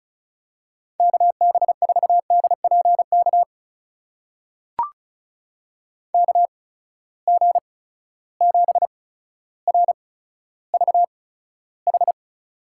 Morse code, heard as a keyed tone, about 36 wpm, spaced as for 15 wpm, 700 Hz. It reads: KB4DPK KG7RVH
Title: USA Call Signs - 6 Characters - 8x Character Spacing - 35wpm